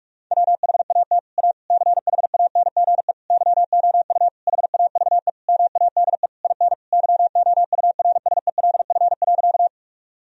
Morse code, heard as a keyed tone, about 41 wpm, 700 Hz. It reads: WHAT A CHANGE YOU HAVE MADE IN YOURSELF.
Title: Sentences from Top 1000 Words - Review Part 1 - 45wpm